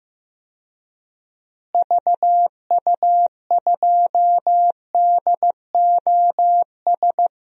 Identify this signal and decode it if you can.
Morse code, heard as a keyed tone, about 15 wpm, 700 Hz. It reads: VU2DOS
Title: DXCC-100 Call Signs - 6 Characters - Repeated 2x - 15wpm